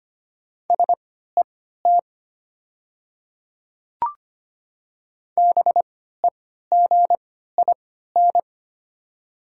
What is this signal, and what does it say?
Morse code, heard as a keyed tone, about 25 wpm, spaced as for 14 wpm, 700 Hz. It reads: SET BEGIN
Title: 3x Character Spacing - Top 300 Words - 25wpm